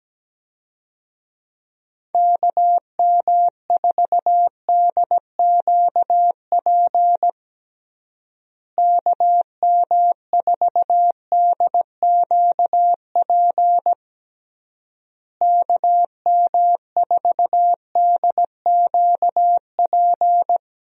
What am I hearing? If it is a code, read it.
Morse code, heard as a keyed tone, about 17 wpm, 700 Hz. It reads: KM4DQP KM4DQP KM4DQP
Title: USA Call Signs - 6 Characters - Repeated 3x - 17wpm